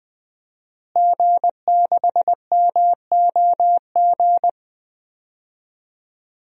Morse code, heard as a keyed tone, about 20 wpm, 700 Hz. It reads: G6MOG
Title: DXCC-100 Call Signs - 5 Characters - Repeated 3x - 20wpm